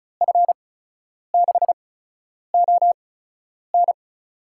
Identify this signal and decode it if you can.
Morse code, heard as a keyed tone, about 34 wpm, spaced as for 10 wpm, 700 Hz. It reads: F6ON
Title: USA Call Signs - 5 Characters - 8x Character Spacing - 35wpm